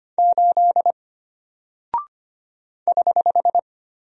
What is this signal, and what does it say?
Morse code, heard as a keyed tone, about 25 wpm, 700 Hz. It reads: 8 <HH>
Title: Single Character (Letter:Number:Punctuation:Prosign) Alt 25wpm